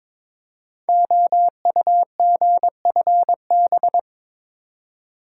Morse code, heard as a keyed tone, about 22 wpm, 700 Hz. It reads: OUGFB
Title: Code Groups - 5 Letters - 22wpm